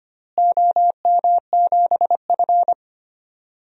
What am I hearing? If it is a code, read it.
Morse code, heard as a keyed tone, about 25 wpm, 700 Hz. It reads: OM7F
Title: DXCC-100 Call Signs - 4 Characters - Repeated 2x - 25wpm